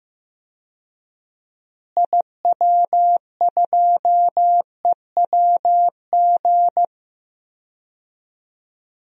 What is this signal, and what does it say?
Morse code, heard as a keyed tone, about 15 wpm, 700 Hz. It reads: IW2EWG
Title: DXCC-100 Call Signs - 6 Characters - Repeated 3x - 15wpm